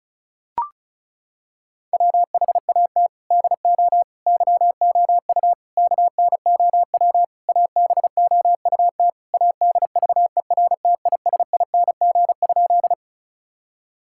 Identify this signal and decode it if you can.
Morse code, heard as a keyed tone, about 36 wpm, 700 Hz. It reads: WHAT DO YOU KNOW ABOUT ADVERTISING?
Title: Sentences from Top 1000 Words - Review Part 6 - 35wpm